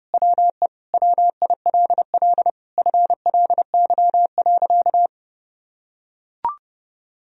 Morse code, heard as a keyed tone, about 32 wpm, 700 Hz. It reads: WE WILL FLY.
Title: Sentences from Top 1000 Words - Review Part 2 - 30wpm